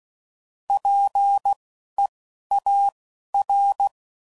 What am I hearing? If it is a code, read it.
Morse code, heard as a keyed tone, about 16 wpm, spaced as for 12 wpm, 800 Hz. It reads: PEAR
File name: Prefixes / By First Letter / Ps